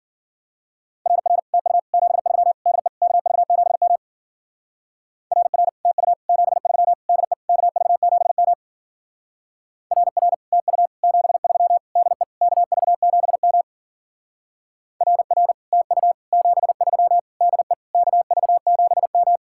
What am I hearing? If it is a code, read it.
Morse code, heard as a keyed tone, about 47 wpm, 700 Hz. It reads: RR TU 73 DE KV7M RR TU 73 DE KV7M RR TU 73 DE KV7M RR TU 73 DE KV7M
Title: Speed-Racing Common QSO Phrases - 40wpm